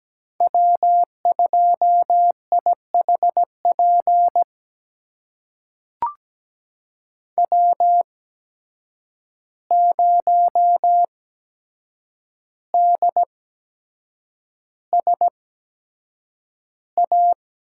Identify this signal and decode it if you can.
Morse code, heard as a keyed tone, about 17 wpm, spaced as for 7 wpm, 700 Hz. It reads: W2IHP W0DSA
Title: USA Call Signs - 5 Characters - 8x Character Spacing - 17wpm